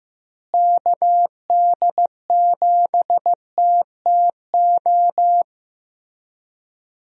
Morse code, heard as a keyed tone, about 15 wpm, 700 Hz. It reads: KD7TTO
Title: USA Call Signs - 6 Characters - Repeated 3x - 15wpm